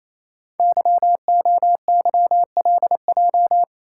Morse code, heard as a keyed tone, about 28 wpm, 700 Hz. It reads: YOYLJ
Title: Code Groups - 5 Letter:Numbers - 28wpm